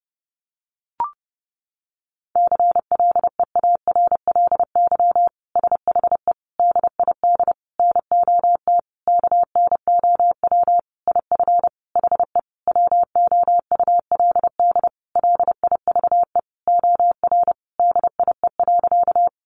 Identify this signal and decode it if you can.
Morse code, heard as a keyed tone, about 30 wpm, 700 Hz. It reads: CLEARLY SHE DID NOT KNOW IF HE WOULD LIVE OR DIE.
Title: Sentences from Top 1300 Words 30wpm